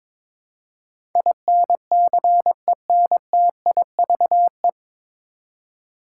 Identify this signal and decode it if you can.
Morse code, heard as a keyed tone, about 22 wpm, 700 Hz. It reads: INCENTIVE
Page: Morse Code Ninja practice 22wpm